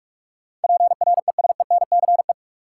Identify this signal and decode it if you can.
Morse code, heard as a keyed tone, about 46 wpm, 700 Hz. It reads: PRESENCE
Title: Sets of 3 Words - New Word Drill - 45wpm